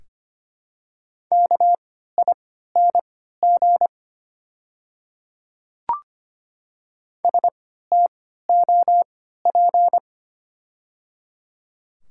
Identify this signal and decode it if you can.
Morse code, heard as a keyed tone, about 25 wpm, spaced as for 14 wpm, 700 Hz. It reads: KING STOP
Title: 3x Character Spacing - Top 500 Words - 25wpm